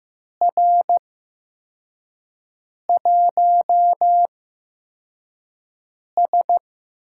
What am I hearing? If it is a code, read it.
Morse code, heard as a keyed tone, about 15 wpm, spaced as for 4 wpm, 700 Hz. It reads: R1S
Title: USA Call Signs - 4 Characters - 8x Character Spacing - 15wpm